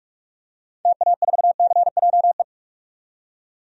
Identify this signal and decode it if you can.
Morse code, heard as a keyed tone, about 44 wpm, 700 Hz. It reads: TA4CJE
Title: DXCC-100 Call Signs - 6 Characters - 45wpm